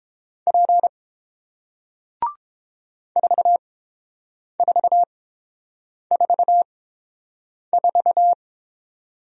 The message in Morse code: P 4444

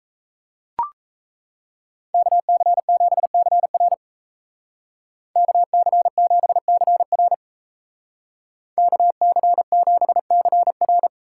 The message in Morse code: KC7CR KC7CR KC7CR

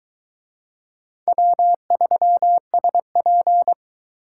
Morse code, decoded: W3SP

